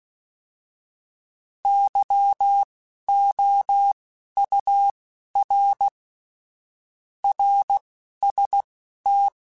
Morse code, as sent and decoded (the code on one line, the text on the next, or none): -.-- --- ..- .-. / .-. ... -
YOUR RST